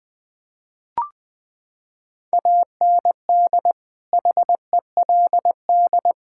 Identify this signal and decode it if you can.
Morse code, heard as a keyed tone, about 20 wpm, 700 Hz. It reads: AND HELD
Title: Sets of 9 Words - 20wpm